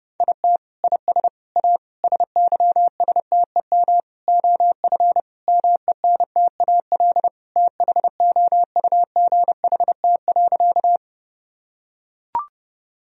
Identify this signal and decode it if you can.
Morse code, heard as a keyed tone, about 32 wpm, 700 Hz. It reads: IT IS A SYSTEM OF MENTAL THOUGHT.
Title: Sentences from Top 1000 Words - Review Part 4 - 30wpm